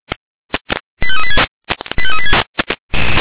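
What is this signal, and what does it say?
Five-tone selective call (ZVEI), heard as quick runs of five tones, tones of about 70 ms. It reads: 85386 85386
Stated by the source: ZVEI five-tone selective call received on an SDR.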